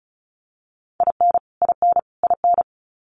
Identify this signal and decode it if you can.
Morse code, heard as a keyed tone, about 34 wpm, 700 Hz. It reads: IN IN IN
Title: Top 500 Words Repeated 35wpm